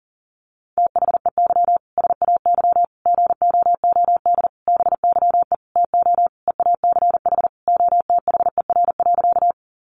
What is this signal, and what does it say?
Morse code, heard as a keyed tone, about 37 wpm, 700 Hz. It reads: THEY SAY GOOD BYE TO EACH OTHER.
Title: Easy Phrases 40wpm